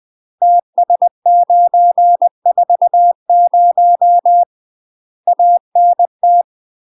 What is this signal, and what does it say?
Morse code, heard as a keyed tone, about 20 wpm, 700 Hz. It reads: TS940 ANT
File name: UK MASTER QSOs TRAINING / 20WPM / UK MASTER QSOs 0060